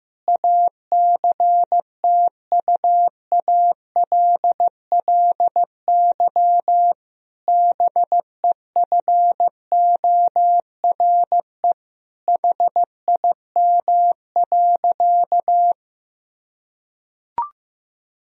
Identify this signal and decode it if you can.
Morse code, heard as a keyed tone, about 15 wpm, 700 Hz. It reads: ACTUALLY BEFORE HIM.
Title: Sentences from Top 1000 Words - Review Part 4 - 15wpm